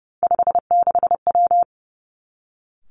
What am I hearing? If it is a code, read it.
Morse code, heard as a keyed tone, about 32 wpm, 700 Hz. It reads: 56W
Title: Three Challenging Letter-Number Combinations 30wpm